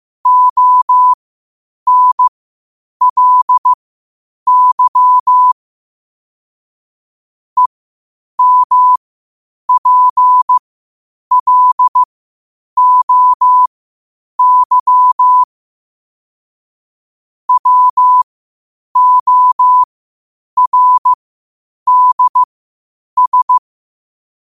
Morse code, heard as a keyed tone, about 15 wpm, spaced as for 8 wpm, 1000 Hz. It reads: ONLY EMPLOY WORDS
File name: 8wpm